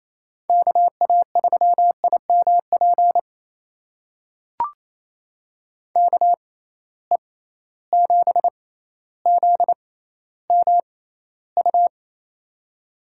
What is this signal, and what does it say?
Morse code, heard as a keyed tone, about 28 wpm, spaced as for 14 wpm, 700 Hz. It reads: KA3IMP KE7ZMU
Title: USA Call Signs - 6 Characters - 6x Character Spacing - 28wpm